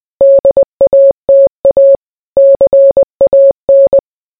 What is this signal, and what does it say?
Morse code, heard as a keyed tone, about 20 wpm, 550 Hz. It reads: DATA CAN